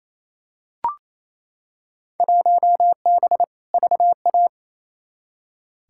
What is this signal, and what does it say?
Morse code, heard as a keyed tone, about 28 wpm, 700 Hz. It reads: 1B VA
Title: Field Day Exchanges Encore - 28wpm